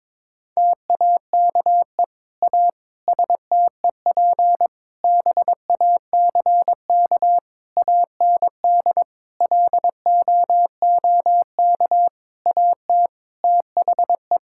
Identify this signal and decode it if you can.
Morse code, heard as a keyed tone, about 22 wpm, 700 Hz. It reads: TAKE A STEP BACK AND LOOK AT THE
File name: practice / 22wpm / Sets of 8 Words - 22wpm